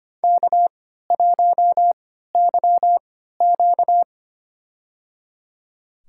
Morse code, heard as a keyed tone, about 25 wpm, spaced as for 14 wpm, 700 Hz. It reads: K1YQ
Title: USA Call Signs - 4 Characters - 3x Character Spacing - 25wpm